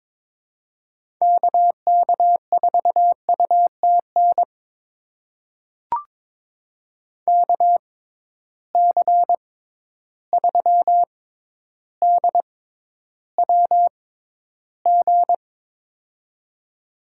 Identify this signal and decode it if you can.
Morse code, heard as a keyed tone, about 22 wpm, spaced as for 11 wpm, 700 Hz. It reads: KK4UTN KC3DWG
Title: USA Call Signs - 6 Characters - 6x Character Spacing - 22wpm